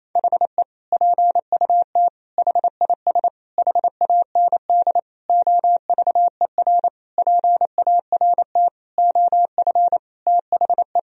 Morse code, heard as a keyed tone, about 28 wpm, 700 Hz. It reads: HE PUT HIS HAND OVER PART OF THE
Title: Sentences from Top 1000 Words - Review Part 2 - 28wpm